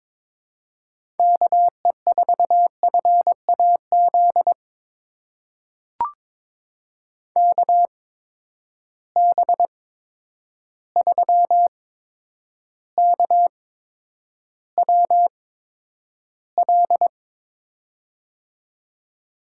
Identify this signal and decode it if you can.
Morse code, heard as a keyed tone, about 22 wpm, spaced as for 9 wpm, 700 Hz. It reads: KE4FAZ KB3KWL
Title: USA Call Signs - 6 Characters - 8x Character Spacing - 22wpm